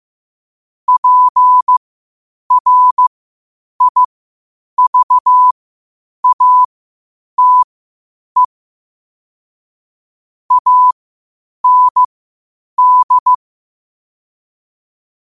Morse code, heard as a keyed tone, about 15 wpm, spaced as for 8 wpm, 1000 Hz. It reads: PRIVATE AND
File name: qotdfiles / 8wpm